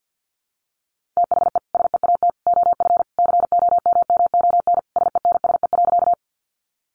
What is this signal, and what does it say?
Morse code, heard as a keyed tone, about 50 wpm, 700 Hz. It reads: THE SEAT OF COMMON SENSE.
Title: Sentences from Top 1500 Words 50wpm